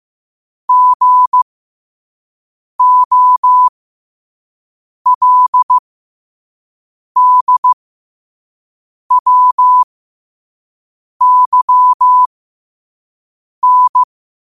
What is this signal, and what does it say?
Morse code, heard as a keyed tone, about 15 wpm, spaced as for 5 wpm, 1000 Hz. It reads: GOLDWYN
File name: qotdfiles / 5wpm